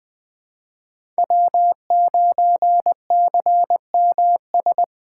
Morse code, heard as a keyed tone, about 20 wpm, 700 Hz. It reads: W9CMS
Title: USA Call Signs - 5 Characters - 8x Character Spacing - 20wpm